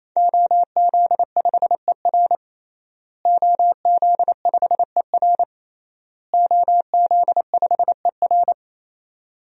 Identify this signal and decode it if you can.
Morse code, heard as a keyed tone, about 27 wpm, 700 Hz. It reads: OZ5ER OZ5ER OZ5ER
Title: DXCC-100 Call Signs - 5 Characters - Repeated 3x - 28wpm